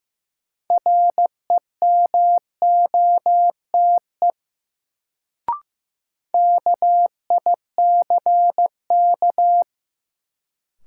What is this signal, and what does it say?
Morse code, heard as a keyed tone, about 16 wpm, 700 Hz. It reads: REMOTE KICK
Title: Sentences from Top 1300 Words - New Word Drill - 15wpm